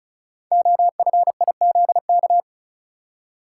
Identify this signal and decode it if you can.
Morse code, heard as a keyed tone, about 36 wpm, 690 Hz. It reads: OFIZK
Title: Code Groups - 5 Letters - 35wpm